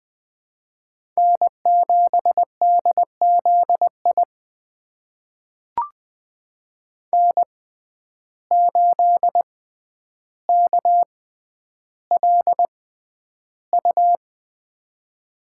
Morse code, heard as a keyed tone, about 20 wpm, spaced as for 10 wpm, 700 Hz. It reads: N7DZI N8KLU
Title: USA Call Signs - 5 Characters - 6x Character Spacing - 20wpm